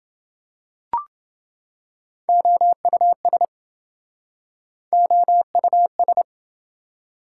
Morse code, heard as a keyed tone, about 28 wpm, 700 Hz. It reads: OUS OUS